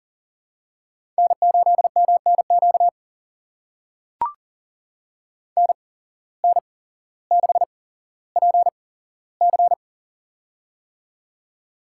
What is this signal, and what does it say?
Morse code, heard as a keyed tone, about 40 wpm, spaced as for 17 wpm, 700 Hz. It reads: N8MNQ NN6PC
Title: USA Call Signs - 5 Characters - 8x Character Spacing - 40wpm